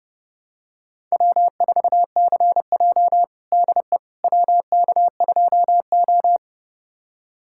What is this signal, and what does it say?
Morse code, heard as a keyed tone, about 32 wpm, 700 Hz. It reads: W4CJ DE WK2O
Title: USA Callsign DE Callsign - 4 Characters - 30wpm